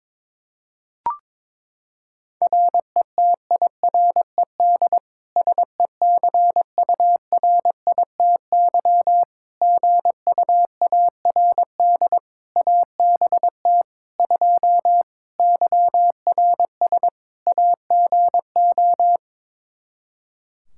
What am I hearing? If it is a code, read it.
Morse code, heard as a keyed tone, about 21 wpm, 700 Hz. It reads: RETIRED SECURITY GUARD ABT 2 YRS AGO
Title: Practice QSO - Exchange 3 - More Info - 22wpm